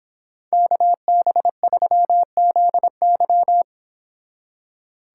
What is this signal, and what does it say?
Morse code, heard as a keyed tone, about 26 wpm, 700 Hz. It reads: KB3ZY